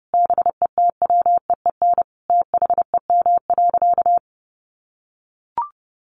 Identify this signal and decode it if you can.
Morse code, heard as a keyed tone, about 30 wpm, 700 Hz. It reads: BETWEEN THEM.
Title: Sentences from Top 400 Words 30wpm